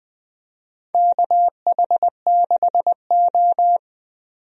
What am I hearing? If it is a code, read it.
Morse code, heard as a keyed tone, about 20 wpm, 700 Hz. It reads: KH6O
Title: DXCC-100 Call Signs - 4 Characters - 20wpm